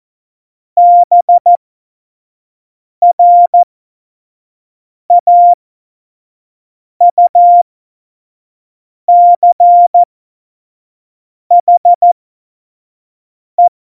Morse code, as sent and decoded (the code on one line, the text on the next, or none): -... .-. .- ..- -.-. .... .
BRAUCHE